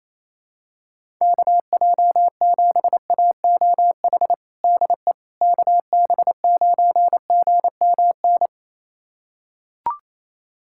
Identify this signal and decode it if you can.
Morse code, heard as a keyed tone, about 28 wpm, 700 Hz. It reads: KJ7AOH DE KB9GMN
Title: USA Callsign DE Callsign - 6 Characters - 28wpm